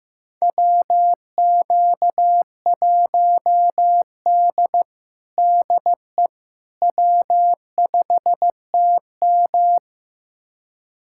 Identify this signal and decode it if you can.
Morse code, heard as a keyed tone, about 15 wpm, 700 Hz. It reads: WQ1D DE W5TM
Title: USA Callsign DE Callsign - 4 Characters - 15wpm